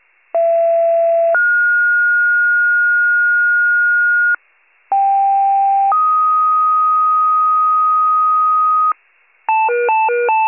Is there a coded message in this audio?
Yes: two-tone page from 0.3 to 8.9 s, heard as a one-second tone, then a three-second tone, about 670 then 1430 Hz, about 790 then 1250 Hz, with noise.